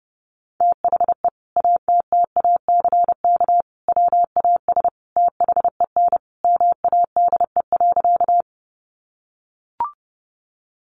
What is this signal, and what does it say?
Morse code, heard as a keyed tone, about 30 wpm, 700 Hz. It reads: THE ATTACK WAS THEN MADE.
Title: Sentences from Top 1500 Words 30wpm